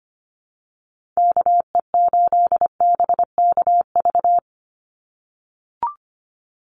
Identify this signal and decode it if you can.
Morse code, heard as a keyed tone, about 25 wpm, 700 Hz. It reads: KE8BKV
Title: USA Call Signs 25wpm